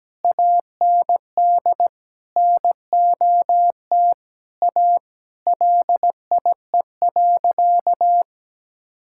Morse code, heard as a keyed tone, about 17 wpm, 700 Hz. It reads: AND NOT A LIE.
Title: Sentences from Top 1600 Words 17wpm